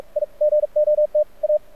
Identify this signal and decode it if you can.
Morse code, heard as a keyed tone, about 46 wpm, 600 Hz, with radio noise. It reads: I GOT A
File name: archived / CRSnet